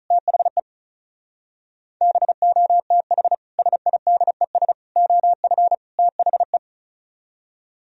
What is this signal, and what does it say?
Morse code, heard as a keyed tone, about 45 wpm, 700 Hz. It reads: THE BOTH SIDES OF THE